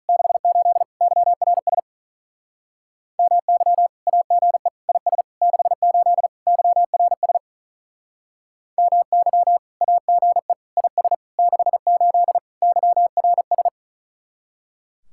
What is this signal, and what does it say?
Morse code, heard as a keyed tone, about 43 wpm, 700 Hz. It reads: 68 YRS MY AGE IS 68 YRS MY AGE IS 68 YRS
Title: Speed-Racing Common QSO Phrases Encore - 35wpm